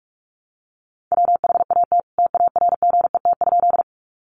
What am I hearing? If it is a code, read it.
Morse code, heard as a keyed tone, about 44 wpm, 700 Hz. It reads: WHAT TARGET?
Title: Sentences from Top 1200 Words 45wpm